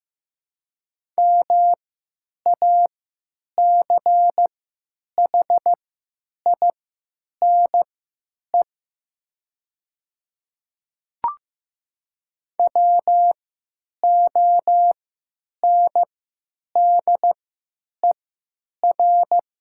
Morse code, heard as a keyed tone, about 15 wpm, spaced as for 9 wpm, 700 Hz. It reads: MACHINE WONDER
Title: 3x Character Spacing - Top 500 Words - 15wpm